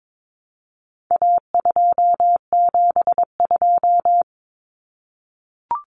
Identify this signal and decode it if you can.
Morse code, heard as a keyed tone, about 22 wpm, 700 Hz. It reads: A272
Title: Common Cut Numbers - 4-Digit 22wpm